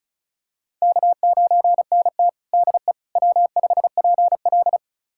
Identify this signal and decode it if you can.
Morse code, heard as a keyed tone, about 36 wpm, 700 Hz. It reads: K9NT DE W5PL
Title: USA Callsign DE Callsign - 4 Characters - 35wpm